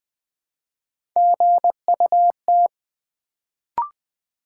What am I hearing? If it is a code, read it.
Morse code, heard as a keyed tone, about 20 wpm, 700 Hz. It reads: GUT
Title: Build A Word - 1 Letter At A Time - 3-Letter Words - 20wpm